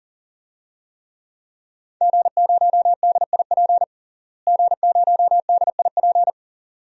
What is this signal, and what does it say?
Morse code, heard as a keyed tone, about 40 wpm, 700 Hz. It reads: G0DIP G0DIP
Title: DXCC-100 Call Signs - 5 Characters - Repeated 2x - 40wpm